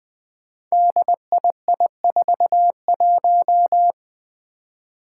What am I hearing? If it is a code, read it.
Morse code, heard as a keyed tone, about 20 wpm, 700 Hz. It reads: DII41
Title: Five Challenging Letter-Number Combinations 20wpm